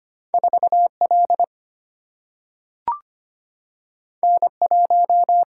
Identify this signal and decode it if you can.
Morse code, heard as a keyed tone, about 25 wpm, 700 Hz. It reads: 4 L N 1
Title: Single-Letter Single-Number 25wpm